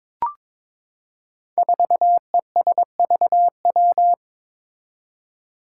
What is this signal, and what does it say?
Morse code, heard as a keyed tone, about 21 wpm, 700 Hz. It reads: 4ESVW
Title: Five Challenging Letter-Number Combinations 22wpm